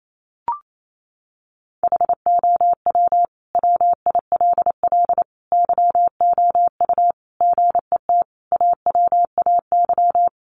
Morse code, heard as a keyed tone, about 28 wpm, 700 Hz. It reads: HOW WILL YOU GET AWAY
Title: Sentences from Top 1500 Words 28wpm